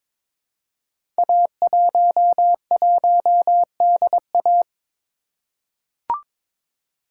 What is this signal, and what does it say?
Morse code, heard as a keyed tone, about 22 wpm, 700 Hz. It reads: A11DA
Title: Code Groups - 5 Letter:Numbers - 22wpm